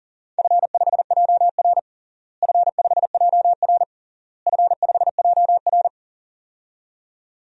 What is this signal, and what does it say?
Morse code, heard as a keyed tone, about 37 wpm, 700 Hz. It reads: F5JR F5JR F5JR
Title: DXCC-100 Call Signs - 4 Characters - Repeated 3x - 40wpm